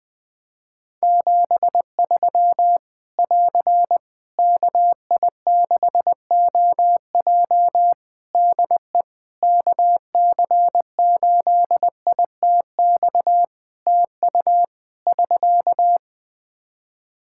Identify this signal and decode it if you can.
Morse code, heard as a keyed tone, about 21 wpm, 700 Hz. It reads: 73 <AR> KI6OJ DE KC8ITX TU <SK>
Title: K1USN SST QSO Practice 20wpm